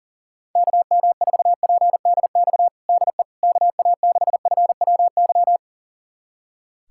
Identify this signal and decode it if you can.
Morse code, heard as a keyed tone, about 37 wpm, 700 Hz. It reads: KM4PDX DE KA6FWY
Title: USA Callsign DE Callsign - 6 Characters - 40wpm